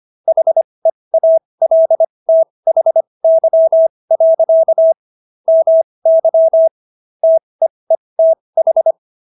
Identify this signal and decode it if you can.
Morse code, heard as a keyed tone, about 25 wpm, spaced as for 20 wpm, 650 Hz. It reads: HEALTHY. MY TEETH